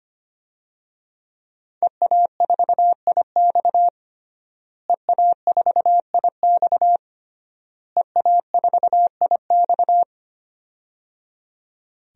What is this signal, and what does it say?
Morse code, heard as a keyed tone, about 25 wpm, 700 Hz. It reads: EA4IX EA4IX EA4IX
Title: DXCC-100 Call Signs - 5 Characters - Repeated 3x - 25wpm